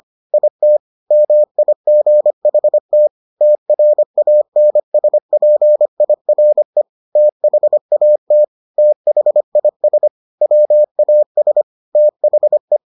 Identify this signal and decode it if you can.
Morse code, heard as a keyed tone, about 25 wpm, 600 Hz. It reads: IT MIGHT TRANSPIRE THAT THIS WAS THE